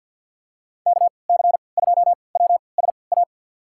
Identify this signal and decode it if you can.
Morse code, heard as a keyed tone, about 46 wpm, spaced as for 27 wpm, 690 Hz. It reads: KC2WSA
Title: USA Call Signs - 3x Character Spacing - 50wpm